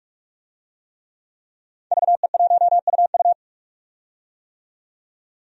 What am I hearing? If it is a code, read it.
Morse code, heard as a keyed tone, about 44 wpm, 700 Hz. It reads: VE1UU
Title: DXCC-100 Call Signs - 5 Characters - 45wpm